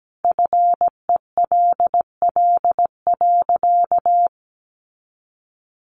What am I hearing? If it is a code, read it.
Morse code, heard as a keyed tone, about 17 wpm, 700 Hz. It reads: FELL.
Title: A Sound Of Thunder 17wpm